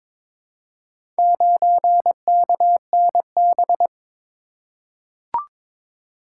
Morse code, heard as a keyed tone, about 22 wpm, 700 Hz. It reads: 9KNB
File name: practice / 22wpm / Four Letter-Number Combinations - 22wpm